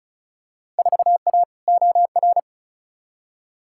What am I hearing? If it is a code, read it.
Morse code, heard as a keyed tone, about 36 wpm, 700 Hz. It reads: 4A OR
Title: Field Day Exchanges - 35wpm